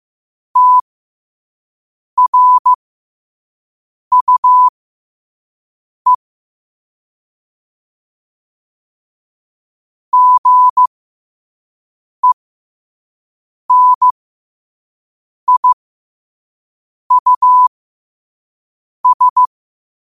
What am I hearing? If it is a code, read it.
Morse code, heard as a keyed tone, about 15 wpm, spaced as for 5 wpm, 1000 Hz. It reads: TRUE GENIUS
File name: qotdfiles / 5wpm